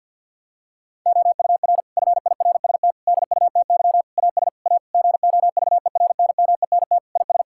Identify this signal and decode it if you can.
Morse code, heard as a keyed tone, about 50 wpm, 700 Hz. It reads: OUR FIRST DUTY AS A GOVERNMENT IS
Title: Sentences from Top 1600 Words 50wpm